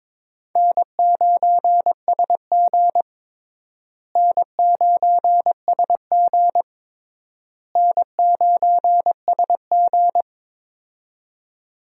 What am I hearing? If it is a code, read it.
Morse code, heard as a keyed tone, about 22 wpm, 700 Hz. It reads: N9SG N9SG N9SG
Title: USA Call Signs - 4 Characters - Repeated 3x - 22wpm